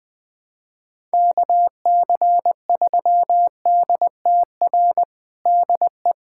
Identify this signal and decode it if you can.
Morse code, heard as a keyed tone, about 21 wpm, 700 Hz. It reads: KC3DTR DE
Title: USA Callsign DE Callsign 20wpm